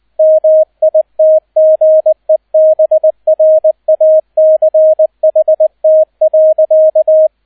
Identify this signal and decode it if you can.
Morse code, heard as a keyed tone, about 20 wpm, 630 Hz, with radio noise. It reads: MITGEBRACHT.
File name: 002 Zweiter Streich 20wpm